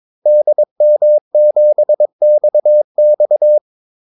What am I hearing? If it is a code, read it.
Morse code, heard as a keyed tone, about 22 wpm, 600 Hz. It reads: DM7XX
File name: Callsigns-22wpm